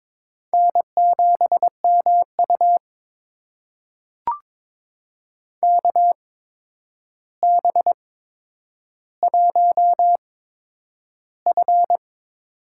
Morse code, heard as a keyed tone, about 22 wpm, spaced as for 9 wpm, 700 Hz. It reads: N7MU KB1F
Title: USA Call Signs - 8x Character Spacing - 22wpm